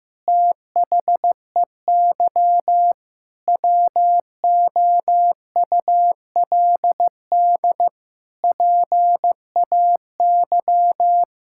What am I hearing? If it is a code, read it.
Morse code, heard as a keyed tone, about 15 wpm, 700 Hz. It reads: THEY WOULD PAY
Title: Sentences from Top 1000 Words - Review Part 5 - 15wpm